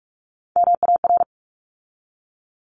Morse code, heard as a keyed tone, about 46 wpm, 700 Hz. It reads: MAR
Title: ARRL Sections 45wpm